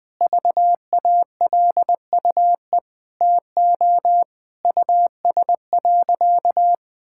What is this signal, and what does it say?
Morse code, heard as a keyed tone, about 20 wpm, 700 Hz. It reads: VALUE TO US.
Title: Sentences from Top 1000 Words - Review Part 5 - 20wpm